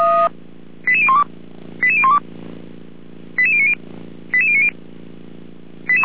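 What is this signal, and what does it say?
Five-tone selective call (ZVEI), heard as quick runs of five tones, tones of about 70 ms. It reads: 88012 88012 88090 88090
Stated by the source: ZVEI five-tone selective call 88459 received on SDR.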